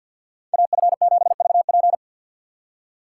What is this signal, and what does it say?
Morse code, heard as a keyed tone, about 64 wpm, 700 Hz. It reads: AF7VP